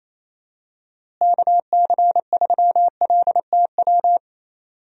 Morse code, heard as a keyed tone, about 28 wpm, 700 Hz. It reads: KC3LTW
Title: USA Call Signs - 6 Characters - Repeated 3x - 28wpm